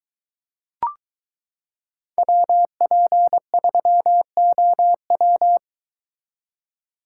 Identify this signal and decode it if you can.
Morse code, heard as a keyed tone, about 22 wpm, 700 Hz. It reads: WP3OW